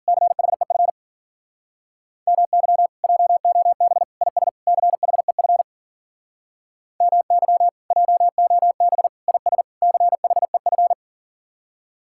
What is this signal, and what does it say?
Morse code, heard as a keyed tone, about 50 wpm, 700 Hz. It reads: CHEF MY JOB IS CHEF MY JOB IS CHEF